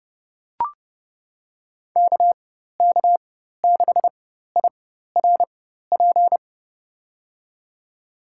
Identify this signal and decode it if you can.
Morse code, heard as a keyed tone, about 30 wpm, spaced as for 14 wpm, 700 Hz. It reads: KK6IRP
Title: USA Call Signs - 6 Characters - 4x Character Spacing - 30wpm